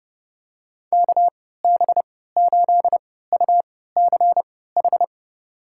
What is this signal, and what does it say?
Morse code, heard as a keyed tone, about 32 wpm, spaced as for 17 wpm, 700 Hz. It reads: KB8UCH
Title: USA Call Signs - 6 Characters - 3x Character Spacing - 30wpm